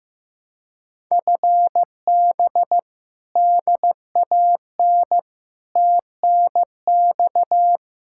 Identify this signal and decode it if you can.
Morse code, heard as a keyed tone, about 15 wpm, 700 Hz. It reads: FB DAN TNX
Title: Speed-Racing Common QSO Phrases Encore - 15wpm